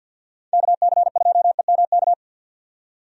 Morse code, heard as a keyed tone, about 50 wpm, 700 Hz. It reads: KC2EMK
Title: USA Call Signs - 6 Characters - 50wpm